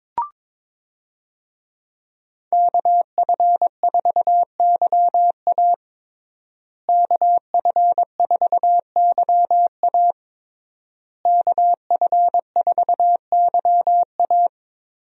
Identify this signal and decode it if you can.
Morse code, heard as a keyed tone, about 22 wpm, 700 Hz. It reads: KF4YA KF4YA KF4YA
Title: USA Call Signs - 5 Characters - Repeated 3x - 22wpm